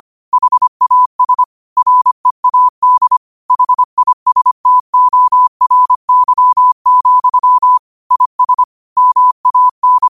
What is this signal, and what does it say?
Morse code, heard as a keyed tone, about 25 wpm, 1000 Hz. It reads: HAS READ HISTORY, IS MAN
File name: qotdfiles / 25wpm